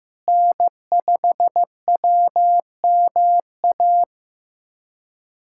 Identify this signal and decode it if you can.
Morse code, heard as a keyed tone, about 15 wpm, 700 Hz. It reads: N5WMA
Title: USA Callsign DE Callsign - 5 Characters - 15wpm